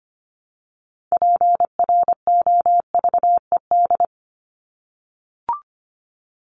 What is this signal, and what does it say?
Morse code, heard as a keyed tone, about 25 wpm, 700 Hz. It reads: PROVED
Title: ED Suffix - Top 100 Words with Base Words 25wpm